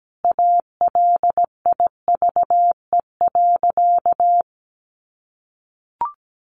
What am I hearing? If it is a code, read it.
Morse code, heard as a keyed tone, about 17 wpm, 700 Hz. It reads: ALIVE.
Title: A Sound Of Thunder 17wpm